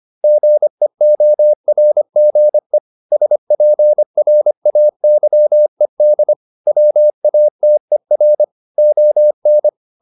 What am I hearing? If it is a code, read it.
Morse code, heard as a keyed tone, about 25 wpm, 600 Hz. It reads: GEORGE SPRAYED WATER ON